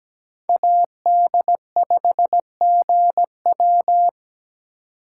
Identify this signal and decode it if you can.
Morse code, heard as a keyed tone, about 21 wpm, 700 Hz. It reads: AD5GW
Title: Speed Racing - USA 2x2 Call Signs 17wpm